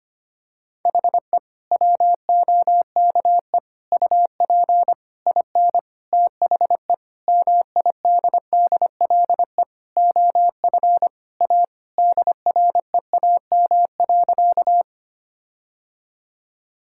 Morse code, heard as a keyed tone, about 25 wpm, 700 Hz. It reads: HE WOKE UP IN THE MIDDLE OF A DREAM.
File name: practice / 25wpm / Sentences with 4-Word Phrases - 25wpm